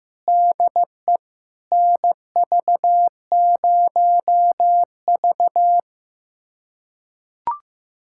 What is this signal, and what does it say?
Morse code, heard as a keyed tone, about 15 wpm, 700 Hz. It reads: DE NV0V
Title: USA Callsign DE Callsign - 4 Characters - 15wpm